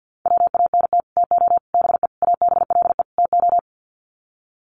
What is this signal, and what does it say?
Morse code, heard as a keyed tone, about 48 wpm, 700 Hz. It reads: WANT TO BE ABLE TO
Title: Sets of 5 Words 50wpm